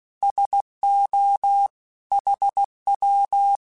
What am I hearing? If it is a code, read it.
Morse code, heard as a keyed tone, about 16 wpm, 800 Hz. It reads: SO HW